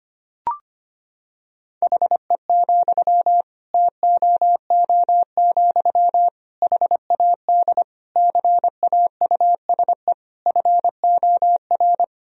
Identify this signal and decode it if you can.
Morse code, heard as a keyed tone, about 25 wpm, 700 Hz. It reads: HE, TOO, HAD CAUSE FOR
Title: Sentences from Top 1000 Words - Review Part 3 - 25wpm